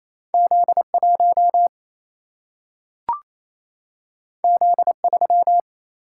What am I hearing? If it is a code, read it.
Morse code, heard as a keyed tone, about 28 wpm, 700 Hz. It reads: Z1 Z3
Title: Single-Letter Single-Number 28wpm